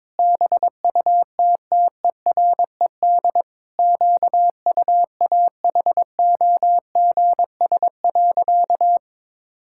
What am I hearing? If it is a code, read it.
Morse code, heard as a keyed tone, about 22 wpm, 700 Hz. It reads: BUTTERED QUAHOGS.
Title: Pangram Challenge v2 - 22wpm